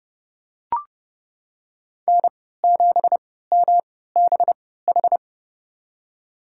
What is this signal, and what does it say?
Morse code, heard as a keyed tone, about 32 wpm, spaced as for 17 wpm, 700 Hz. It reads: N7MBH
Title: USA Call Signs - 3x Character Spacing - 30wpm